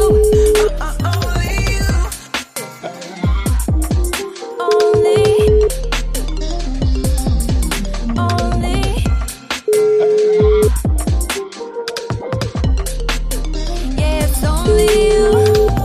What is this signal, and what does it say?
Ringback tone (Europe, 425 Hz), with noise.